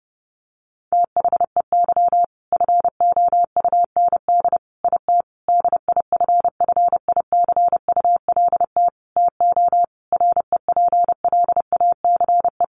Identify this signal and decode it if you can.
Morse code, heard as a keyed tone, about 30 wpm, 700 Hz. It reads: THEY FOUND IT DIFFICULT TO REPLACE
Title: Sentences from Top 1400 Words 30wpm